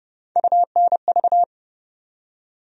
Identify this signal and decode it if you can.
Morse code, heard as a keyed tone, about 30 wpm, 700 Hz. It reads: UNV
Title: USA Call Sign Postfixes - 3 Letters 30wpm